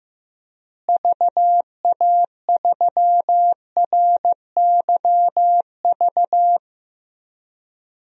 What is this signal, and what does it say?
Morse code, heard as a keyed tone, about 15 wpm, 700 Hz. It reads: VA3RYV
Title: DXCC-100 Call Signs - 6 Characters - Repeated 2x - 15wpm